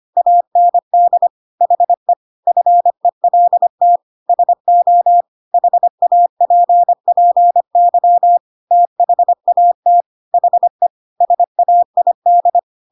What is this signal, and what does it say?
Morse code, heard as a keyed tone, about 25 wpm, 700 Hz. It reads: AND HE FELT SO HAPPY THAT HE SAID